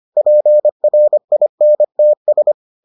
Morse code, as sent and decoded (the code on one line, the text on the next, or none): .--. .-. .. -. - ...
PRINTS